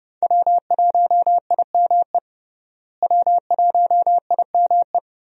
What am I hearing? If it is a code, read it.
Morse code, heard as a keyed tone, about 32 wpm, 700 Hz. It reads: W1IME W1IME
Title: USA Call Signs - 5 Characters - Repeated 2x - 30wpm